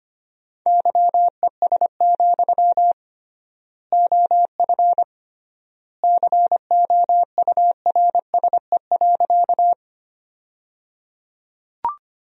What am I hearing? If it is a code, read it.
Morse code, heard as a keyed tone, about 25 wpm, 700 Hz. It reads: YES, OF COURSE.
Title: 3x Word-Spacing - Sentences from Top 500 Words - 25wpm